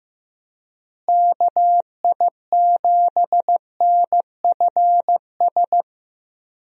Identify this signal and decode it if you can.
Morse code, heard as a keyed tone, about 16 wpm, 700 Hz. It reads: KI7NFS
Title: USA Call Signs - 6 Characters - Repeated 2x - 15wpm